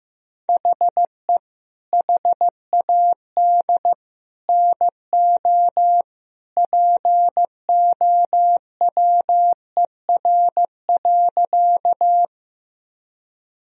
Morse code, heard as a keyed tone, about 16 wpm, 700 Hz. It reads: HE HAD NO POWER.
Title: Sentences from Top 1000 Words - Review Part 2 - 15wpm